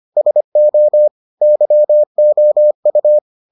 Morse code, heard as a keyed tone, about 25 wpm, 600 Hz. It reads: SO YOU